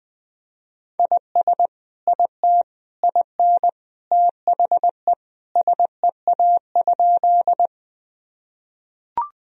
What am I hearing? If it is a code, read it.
Morse code, heard as a keyed tone, about 20 wpm, 700 Hz. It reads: IS IT IN THE SEA?
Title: Sentences from Top 1000 Words - Review Part 1 - 20wpm